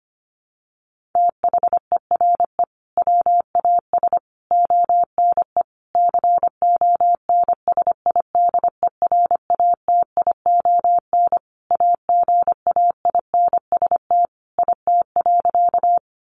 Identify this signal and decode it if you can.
Morse code, heard as a keyed tone, about 25 wpm, 700 Hz. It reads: THERE WAS ONE CONSIDERATION AGAINST IT.
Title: Sentences from Top 1500 Words 25wpm